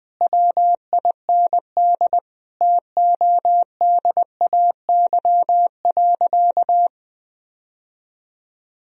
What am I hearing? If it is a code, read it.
Morse code, heard as a keyed tone, about 21 wpm, 700 Hz. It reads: WIND TODAY.
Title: Sentences from Top 1000 Words - Review Part 2 - 20wpm